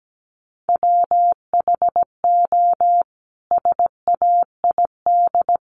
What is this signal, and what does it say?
Morse code, heard as a keyed tone, about 17 wpm, 700 Hz. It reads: WHO SAID
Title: Sentences from Top 300 Words 17wpm